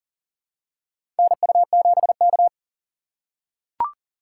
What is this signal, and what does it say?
Morse code, heard as a keyed tone, about 40 wpm, 700 Hz. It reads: NU7K
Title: USA Call Signs - 4 Characters - 4x Character Spacing - 40wpm